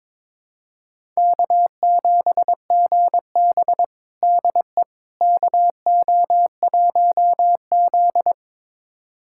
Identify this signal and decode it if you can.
Morse code, heard as a keyed tone, about 22 wpm, 700 Hz. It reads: K7GB DE KO1Z
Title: USA Callsign DE Callsign - 4 Characters - 22wpm